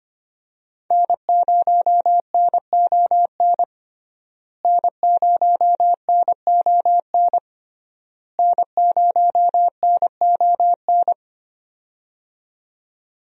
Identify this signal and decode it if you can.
Morse code, heard as a keyed tone, about 25 wpm, 700 Hz. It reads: N0NON N0NON N0NON
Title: USA Call Signs - 5 Characters - Repeated 3x - 25wpm